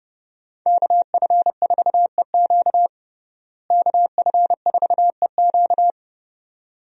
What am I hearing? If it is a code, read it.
Morse code, heard as a keyed tone, about 30 wpm, 700 Hz. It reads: KF4EQ KF4EQ
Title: USA Call Signs - 5 Characters - Repeated 2x - 30wpm